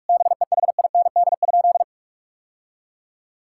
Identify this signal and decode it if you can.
Morse code, heard as a keyed tone, about 41 wpm, 700 Hz. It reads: BEHIND?
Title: Sentences from Top 1000 Words - Review Part 5 - 45wpm